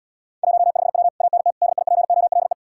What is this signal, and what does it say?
Morse code, heard as a keyed tone, about 79 wpm, 700 Hz. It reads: 15W ANT DIPOLE